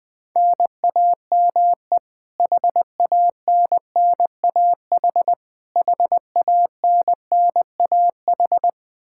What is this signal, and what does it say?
Morse code, heard as a keyed tone, about 21 wpm, 700 Hz. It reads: NAME HANNAH HANNAH
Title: Practice QSO - Exchange 1 - Beginning - 20wpm